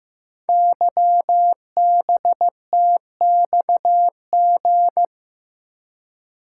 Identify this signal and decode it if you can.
Morse code, heard as a keyed tone, about 15 wpm, 700 Hz. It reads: YBTXG
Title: Code Groups - 5 Letters - 15wpm